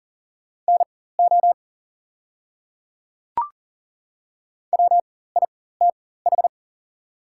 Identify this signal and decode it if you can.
Morse code, heard as a keyed tone, about 40 wpm, spaced as for 19 wpm, 700 Hz. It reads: NO WITH
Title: 4x Character Spacing - Top 100 Words - 40wpm